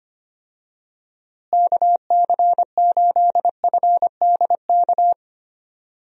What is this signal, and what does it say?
Morse code, heard as a keyed tone, about 25 wpm, 700 Hz. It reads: KC8FDK
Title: USA Call Signs - 6 Characters - Repeated 3x - 25wpm